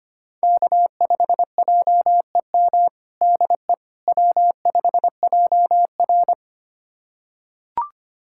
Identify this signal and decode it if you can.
Morse code, heard as a keyed tone, about 25 wpm, 700 Hz. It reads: K5JEM DE W5JR
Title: USA Callsign DE Callsign 25wpm